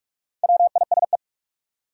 Morse code, heard as a keyed tone, about 43 wpm, 700 Hz. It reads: WISE
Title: Sentences from Top 1600 Words - New Word Drill Alt - 45wpm